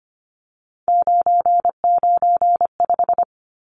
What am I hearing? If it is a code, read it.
Morse code, heard as a keyed tone, about 25 wpm, 700 Hz. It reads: 995
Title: Numbers - 3-Digit 25wpm